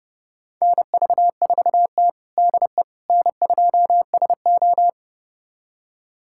Morse code, heard as a keyed tone, about 30 wpm, 700 Hz. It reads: NV4T DE N2SO
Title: USA Callsign DE Callsign - 4 Characters - 30wpm